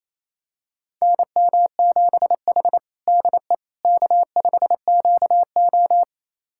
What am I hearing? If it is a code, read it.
Morse code, heard as a keyed tone, about 28 wpm, 700 Hz. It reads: NM7H DE K5QO
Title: USA Callsign DE Callsign - 4 Characters - 28wpm